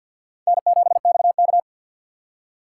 Morse code, heard as a keyed tone, about 53 wpm, 700 Hz. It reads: N7XK
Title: USA Call Signs - 4x Character Spacing - 50wpm